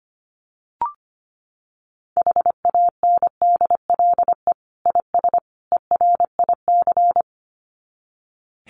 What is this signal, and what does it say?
Morse code, heard as a keyed tone, about 25 wpm, 700 Hz. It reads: HANDLE IS ERIC
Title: Common QSO Phrases 25wpm